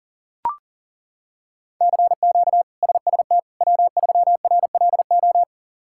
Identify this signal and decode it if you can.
Morse code, heard as a keyed tone, about 40 wpm, 690 Hz. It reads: CQ SST W3RLO
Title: K1USN SST QSO Practice - 40wpm